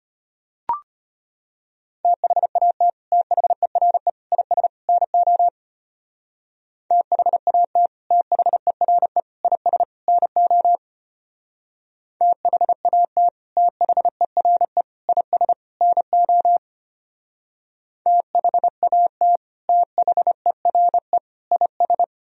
Morse code, heard as a keyed tone, about 30 wpm, 700 Hz. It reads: THAT THERE IS NO THAT THERE IS NO THAT THERE IS NO THAT THERE IS
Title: Speed-Racing - Sets of 4 Words 25wpm